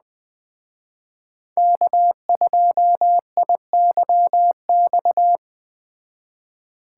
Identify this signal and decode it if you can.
Morse code, heard as a keyed tone, about 21 wpm, 700 Hz. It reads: K2IYX